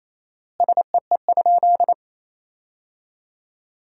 Morse code, heard as a keyed tone, about 28 wpm, 700 Hz. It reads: SEE?
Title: 4x Word-Spacing - Sentences from Top 100 Words - 28wpm